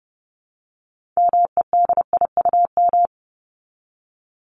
Morse code, heard as a keyed tone, about 30 wpm, 700 Hz. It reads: MEDIUM
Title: Top 751-1000 Words 30wpm